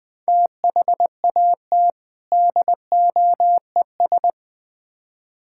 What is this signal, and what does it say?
Morse code, heard as a keyed tone, about 20 wpm, 700 Hz. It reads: THAT DOES
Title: Build A Phrase - 3-Word Phrases - 20wpm